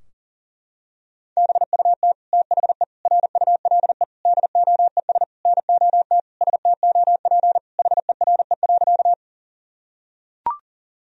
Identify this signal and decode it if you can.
Morse code, heard as a keyed tone, about 40 wpm, 700 Hz. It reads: BUT THE RULE DOES NOT STOP HERE.
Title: Sentences from Top 1000 Words - Review Part 2 - 40wpm